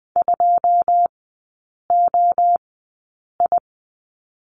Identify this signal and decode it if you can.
Morse code, heard as a keyed tone, about 20 wpm, spaced as for 8 wpm, 700 Hz. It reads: 2OI
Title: 166 Lesson 27e - Number 7 - Three Characters 20wpm